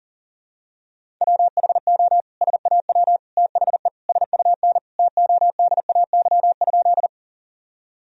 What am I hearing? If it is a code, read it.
Morse code, heard as a keyed tone, about 40 wpm, 700 Hz. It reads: WHO SAW THE SUN TODAY?
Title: Sentences from Top 1000 Words - Review Part 1 - 40wpm